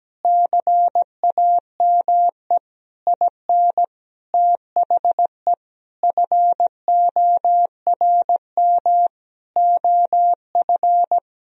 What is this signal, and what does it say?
Morse code, heard as a keyed tone, about 17 wpm, 700 Hz. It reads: CAME IN THE FORM OF
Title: Sentences from Top 1000 Words - Review Part 1 - 17wpm